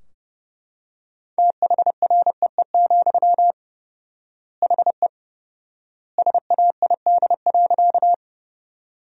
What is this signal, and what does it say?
Morse code, heard as a keyed tone, about 29 wpm, 700 Hz. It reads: THREE, HE SAID.
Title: 4x Word-Spacing - Sentences from Top 200 Words - 30wpm